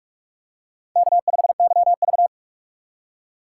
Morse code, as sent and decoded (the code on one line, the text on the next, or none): -.- ..... -.-- ...-
K5YV